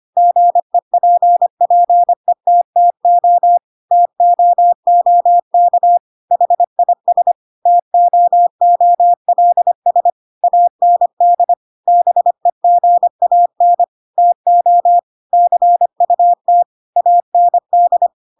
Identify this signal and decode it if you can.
Morse code, heard as a keyed tone, about 25 wpm, 700 Hz. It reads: GEPPETTO TOOK HIS TOOLS AND BEGAN TO CUT AND
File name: Pinocchio / Pinocchio ch3 25 25 ik1hge